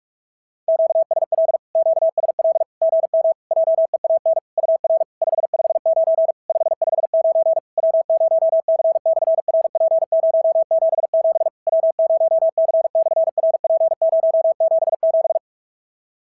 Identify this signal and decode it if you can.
Morse code, heard as a keyed tone, about 42 wpm, 690 Hz. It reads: QSL QSL GM JEAN UR 559 559 W0C/RP077 W0C/RP077
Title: SOTA QSO Practice 45wpm